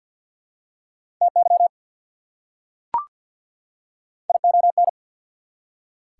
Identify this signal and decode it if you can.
Morse code, heard as a keyed tone, about 48 wpm, 690 Hz. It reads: TY ION
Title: Prefixes and Suffixes 50wpm